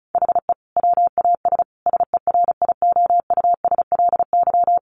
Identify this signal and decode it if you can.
Morse code, heard as a keyed tone, about 34 wpm, 700 Hz. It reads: HE WAS SERIOUSLY
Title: Sentences from Top 1200 Words 35wpm